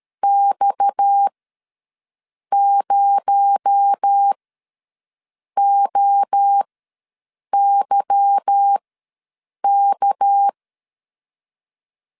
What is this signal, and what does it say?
Morse code, heard as a keyed tone, about 13 wpm, spaced as for 6 wpm, 800 Hz. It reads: X0OYK
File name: LEZIONI / 8 YJ45 / 8 Y ws 12wpm 3409